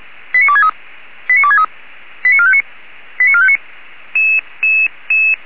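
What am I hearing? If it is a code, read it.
Five-tone selective call (ZVEI), heard as quick runs of five tones, tones of about 70 ms. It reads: 89263 89263 89459 89459
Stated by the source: ZVEI five-tone selective call 89277, received on an SDR.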